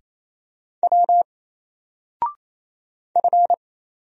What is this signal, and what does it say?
Morse code, heard as a keyed tone, about 27 wpm, 700 Hz. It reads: W F
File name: Single-Letter-Number-28wpm